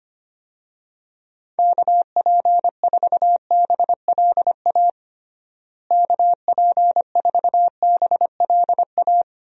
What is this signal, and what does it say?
Morse code, heard as a keyed tone, about 25 wpm, 700 Hz. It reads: KP4BLA KP4BLA
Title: DXCC-100 Call Signs - 6 Characters - Repeated 3x - 25wpm